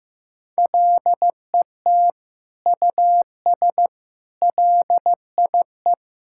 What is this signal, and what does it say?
Morse code, heard as a keyed tone, about 15 wpm, 700 Hz. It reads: LET US LIE
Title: Sentences from Top 1600 Words 15wpm